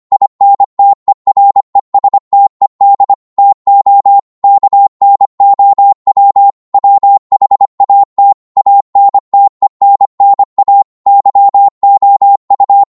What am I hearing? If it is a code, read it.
Morse code, heard as a keyed tone, about 25 wpm, 800 Hz. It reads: INTERESTED TO KNOW WHAT ANTENNA YOU
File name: Morse / qso-25wpm